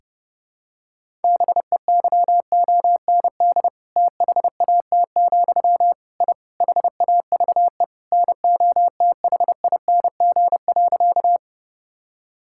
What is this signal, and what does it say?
Morse code, heard as a keyed tone, about 30 wpm, 700 Hz. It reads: BEYOND THAT, I HAVE NOTHING.
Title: Sentences from Top 1000 Words - Review Part 6 - 30wpm